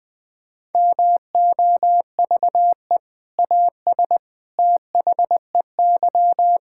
Morse code, heard as a keyed tone, about 20 wpm, 700 Hz. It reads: MOVE AS THEY
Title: Sentences with 2-Word Phrases - 20wpm